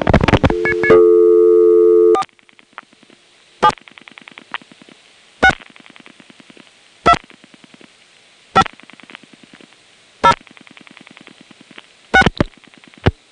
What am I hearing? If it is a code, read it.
DTMF, heard as telephone touch tones, tones of about 70 ms. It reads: 4 7 6 2 5 0 6
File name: phonetrips / SmyrnaGA5HQ